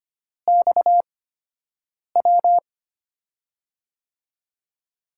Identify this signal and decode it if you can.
Morse code, heard as a keyed tone, about 25 wpm, spaced as for 7 wpm, 700 Hz. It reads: XW
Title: USA Call Signs - 8x Character Spacing - 25wpm